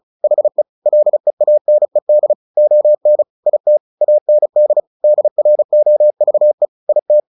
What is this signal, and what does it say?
Morse code, heard as a keyed tone, about 34 wpm, 600 Hz. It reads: HE LEANED ON IT AND DROVE IT
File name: the old man and the sea 2 35wpm